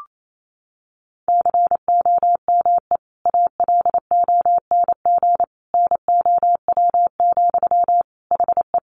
Morse code, heard as a keyed tone, about 28 wpm, 700 Hz. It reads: COME ALONG NOW, HE
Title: Sentences from Top 700 Words 28wpm